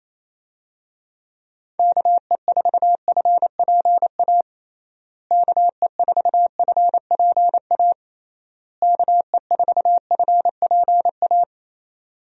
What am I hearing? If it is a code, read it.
Morse code, heard as a keyed tone, about 28 wpm, 700 Hz. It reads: KE4FPA KE4FPA KE4FPA
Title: USA Call Signs - 6 Characters - Repeated 3x - 28wpm